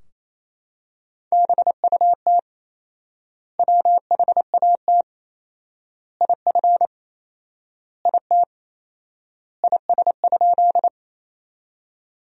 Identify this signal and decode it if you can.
Morse code, heard as a keyed tone, about 27 wpm, 700 Hz. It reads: BUT WHAT IF IT IS?
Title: 4x Word-Spacing - Sentences from Top 100 Words - 28wpm